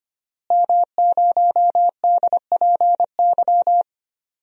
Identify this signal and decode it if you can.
Morse code, heard as a keyed tone, about 25 wpm, 700 Hz. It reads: M0DPY
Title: DXCC-100 Call Signs - 5 Characters - Repeated 3x - 25wpm